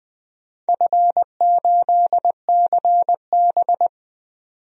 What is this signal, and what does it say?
Morse code, heard as a keyed tone, about 20 wpm, 700 Hz. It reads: F8CB
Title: DXCC-100 Call Signs - 4 Characters - 20wpm